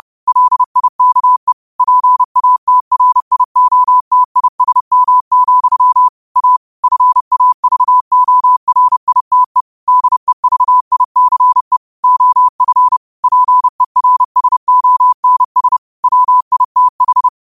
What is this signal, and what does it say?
Morse code, heard as a keyed tone, about 30 wpm, 1000 Hz. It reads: LIKE PATRIOTISM, A FAVORITE DEVICE OF PERSONS WITH